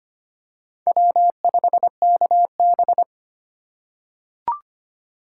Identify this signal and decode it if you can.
Morse code, heard as a keyed tone, about 25 wpm, 700 Hz. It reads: W5KB
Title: USA Call Signs - 4x Character Spacing - 25wpm